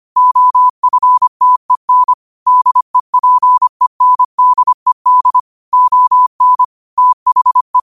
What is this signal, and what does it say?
Morse code, heard as a keyed tone, about 25 wpm, 1000 Hz. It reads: OFTEN DEPENDED ON THE
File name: 25wpm